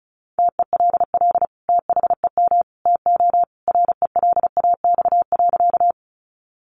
Morse code, heard as a keyed tone, about 34 wpm, 700 Hz. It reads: TELL THEM TO RELAX.
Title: Sentences from Top 1400 Words 35wpm